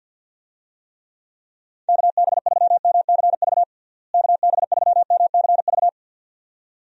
Morse code, heard as a keyed tone, about 50 wpm, 700 Hz. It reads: KB3MCV KB3MCV
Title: USA Call Signs - 6 Characters - Repeated 2x - 50wpm